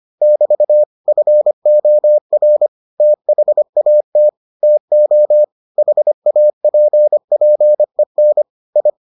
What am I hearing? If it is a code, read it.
Morse code, heard as a keyed tone, about 25 wpm, 600 Hz. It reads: <BT> FOR THAT TO HAPPEN I